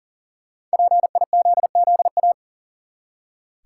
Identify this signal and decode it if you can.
Morse code, heard as a keyed tone, about 40 wpm, 700 Hz. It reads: PIZZA
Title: Five Letter Words - 40wpm